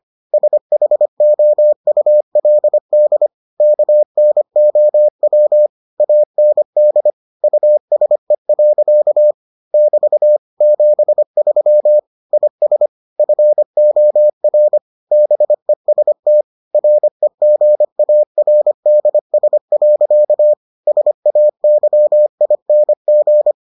Morse code, heard as a keyed tone, about 25 wpm, 600 Hz. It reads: SHOULD KNOW AND USE. <BT> 73 IS FOR BEST REGARDS. SAYING